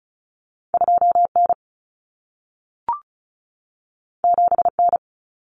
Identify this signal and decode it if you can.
Morse code, heard as a keyed tone, about 36 wpm, 700 Hz. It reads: 2N 7N
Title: Common Cut Numbers - 2-Digit 35wpm